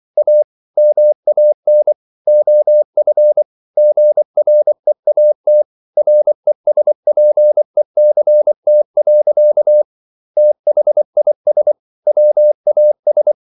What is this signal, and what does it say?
Morse code, heard as a keyed tone, about 24 wpm, 600 Hz. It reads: A MAN OF GREAT RESPECT. THIS WAS